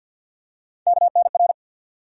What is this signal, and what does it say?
Morse code, heard as a keyed tone, about 53 wpm, 700 Hz. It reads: KNR
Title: USA Call Sign Postfixes - 3 Letters 50wpm